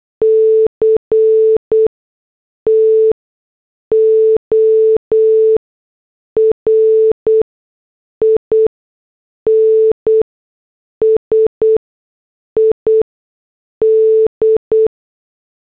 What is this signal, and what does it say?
Morse code, heard as a keyed tone, about 8 wpm, spaced as for 6 wpm, 440 Hz. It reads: CTOR INSID